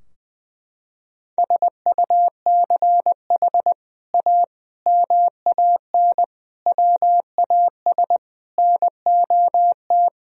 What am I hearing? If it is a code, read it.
Morse code, heard as a keyed tone, about 20 wpm, 700 Hz. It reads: SUCH A MAN WAS NOT
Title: Sentences from Top 1000 Words - Review Part 4 - 20wpm